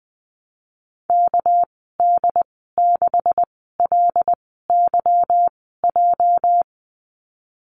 Morse code, heard as a keyed tone, about 19 wpm, spaced as for 14 wpm, 700 Hz. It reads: KD6LYJ
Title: 213 Lesson 34g - Number 8 - Call Signs 20wpm